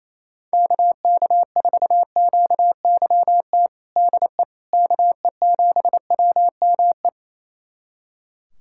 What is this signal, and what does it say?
Morse code, heard as a keyed tone, about 28 wpm, 700 Hz. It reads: KK4QYT DE KE7WME
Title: USA Callsign DE Callsign - 6 Characters - 28wpm